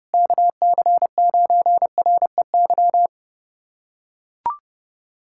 Morse code, heard as a keyed tone, about 29 wpm, 700 Hz. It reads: KC9REY
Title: USA Callsign DE Callsign - 6 Characters - 30wpm